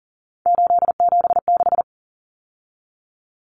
Morse code, heard as a keyed tone, about 40 wpm, 700 Hz. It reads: 876
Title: Numbers - 3-Digit 40wpm